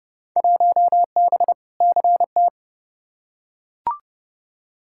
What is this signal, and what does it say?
Morse code, heard as a keyed tone, about 30 wpm, 700 Hz. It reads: 1B CT
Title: Field Day Exchanges Encore - 30wpm